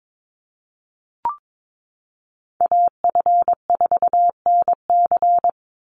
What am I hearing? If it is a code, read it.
Morse code, heard as a keyed tone, about 22 wpm, 700 Hz. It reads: AF4NC
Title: Call Signs - 22wpm